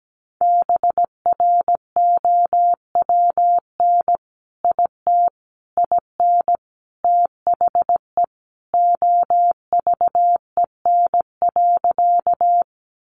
Morse code, heard as a keyed tone, about 17 wpm, 700 Hz. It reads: BROWN IT IN THE OVEN.
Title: Sentences from Top 1200 Words 17wpm